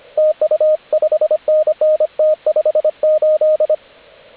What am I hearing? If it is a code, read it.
Morse code, heard as a keyed tone, about 25 wpm, 620 Hz, with radio noise. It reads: TU 5NN T58
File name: AUDIO2019 / HG8DX CW